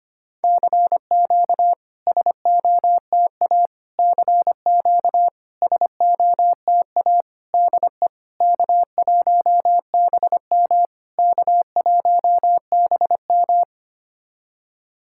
Morse code, heard as a keyed tone, about 25 wpm, 700 Hz. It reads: CQ SOTA CQ SOTA DE K1BM K1BM
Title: SOTA QSO Practice 25wpm